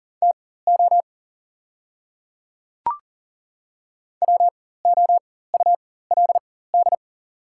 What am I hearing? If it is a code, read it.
Morse code, heard as a keyed tone, about 38 wpm, spaced as for 18 wpm, 690 Hz. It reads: TO WOULD
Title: 4x Character Spacing - Top 100 Words - 40wpm